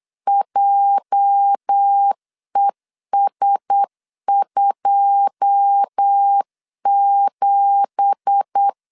Morse code, heard as a keyed tone, about 8 wpm, 800 Hz. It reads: JES27
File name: 8wpm